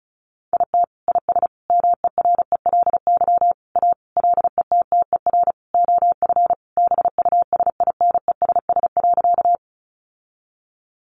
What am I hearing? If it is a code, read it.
Morse code, heard as a keyed tone, about 34 wpm, 700 Hz. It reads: IT IS MERELY A LETTER OF BUSINESS.
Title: Sentences from Top 1400 Words 35wpm